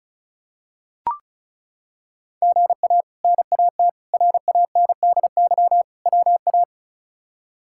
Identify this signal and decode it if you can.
Morse code, heard as a keyed tone, about 34 wpm, 700 Hz. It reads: GA NAT RANDY WA
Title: K1USN SST QSO Practice - 35wpm